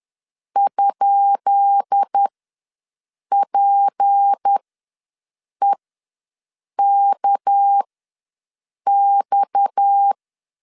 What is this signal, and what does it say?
Morse code, heard as a keyed tone, about 10 wpm, spaced as for 6 wpm, 800 Hz. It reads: ?PEKX
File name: LEZIONI / 10 SIGNS / 10B INT ws 10wpm 4134